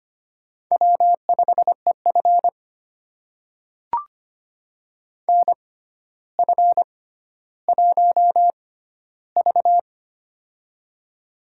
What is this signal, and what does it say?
Morse code, heard as a keyed tone, about 25 wpm, spaced as for 13 wpm, 700 Hz. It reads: W5EF NF1V
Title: USA Call Signs - 4 Characters - 6x Character Spacing - 25wpm